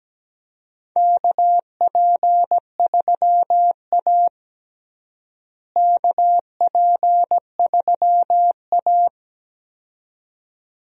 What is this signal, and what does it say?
Morse code, heard as a keyed tone, about 17 wpm, 700 Hz. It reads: KP3A KP3A
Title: USA Call Signs - 4 Characters - Repeated 2x - 17wpm